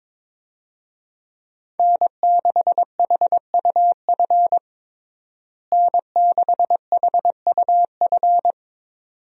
Morse code, heard as a keyed tone, about 22 wpm, 700 Hz. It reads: N6HUF N6HUF
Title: USA Call Signs - 5 Characters - Repeated 2x - 22wpm